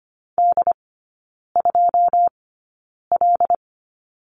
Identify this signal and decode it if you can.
Morse code, heard as a keyed tone, about 25 wpm, spaced as for 9 wpm, 700 Hz. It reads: D2L
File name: Lesson 14e - Number 2 - Three Characters 25wpm